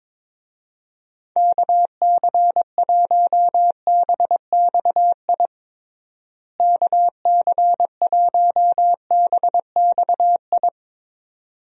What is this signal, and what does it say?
Morse code, heard as a keyed tone, about 22 wpm, 700 Hz. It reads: KC1BXI KC1BXI
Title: USA Call Signs - 6 Characters - Repeated 2x - 22wpm